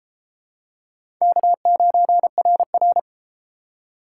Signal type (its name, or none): Morse code